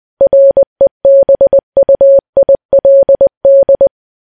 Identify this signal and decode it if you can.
Morse code, heard as a keyed tone, about 20 wpm, 550 Hz. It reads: REBUILD
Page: news-headlines-in-morse-code-at-20-wpm